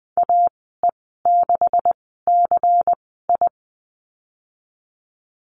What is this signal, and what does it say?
Morse code, heard as a keyed tone, about 20 wpm, spaced as for 15 wpm, 700 Hz. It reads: AE6CI
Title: 179 Lesson 29g - Slash - Call Signs 20wpm